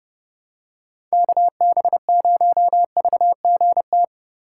Morse code, heard as a keyed tone, about 30 wpm, 700 Hz. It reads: KB0VGT
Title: USA Call Signs - 6 Characters - 2x Character Spacing - 30wpm